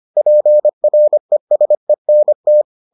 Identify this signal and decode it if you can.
Morse code, heard as a keyed tone, about 25 wpm, 600 Hz. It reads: PRESENT